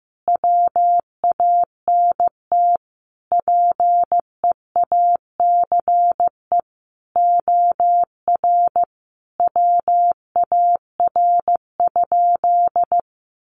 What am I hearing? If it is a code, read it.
Morse code, heard as a keyed tone, about 15 wpm, 700 Hz. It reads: WANT PEACE OR WAR?
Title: Sentences from Top 1500 Words 15wpm